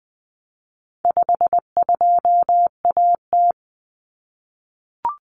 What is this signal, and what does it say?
Morse code, heard as a keyed tone, about 20 wpm, 700 Hz. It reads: 52AT
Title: Common Cut Numbers - 4-Digit 20wpm